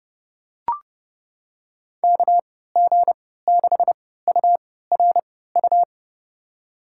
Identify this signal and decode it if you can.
Morse code, heard as a keyed tone, about 31 wpm, spaced as for 17 wpm, 700 Hz. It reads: KG6URU
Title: USA Call Signs - 6 Characters - 3x Character Spacing - 30wpm